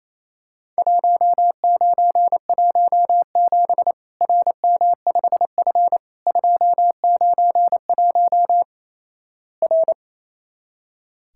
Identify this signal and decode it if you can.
Morse code, heard as a keyed tone, about 28 wpm, 700 Hz. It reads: 1917 RM5F 291 R
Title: Worked All Europe DX Contest - QTC Exchange - 28wpm